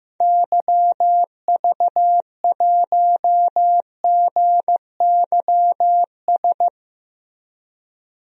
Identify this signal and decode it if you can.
Morse code, heard as a keyed tone, about 15 wpm, 700 Hz. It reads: YV1GYS
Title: DXCC-100 Call Signs - 6 Characters - 15wpm